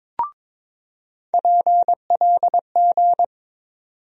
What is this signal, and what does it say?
Morse code, heard as a keyed tone, about 22 wpm, 700 Hz. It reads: PLG
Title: USA Call Sign Postfixes - 3 Letters 22wpm